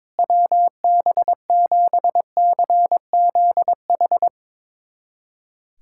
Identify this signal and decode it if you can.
Morse code, heard as a keyed tone, about 22 wpm, 700 Hz. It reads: WB7CZH
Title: USA Callsign DE Callsign 22wpm